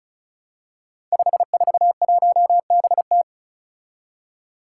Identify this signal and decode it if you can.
Morse code, heard as a keyed tone, about 36 wpm, 700 Hz. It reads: 541BT
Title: Five Challenging Letter-Number Combinations 35wpm